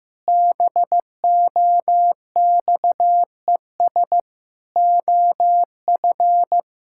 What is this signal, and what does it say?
Morse code, heard as a keyed tone, about 16 wpm, 700 Hz. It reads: BOXES OF
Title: Pangram Challenge v2 - 15wpm